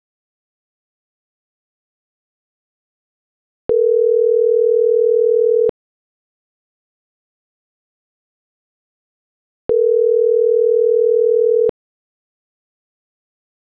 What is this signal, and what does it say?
Ringback tone (North America), 440 + 480 Hz.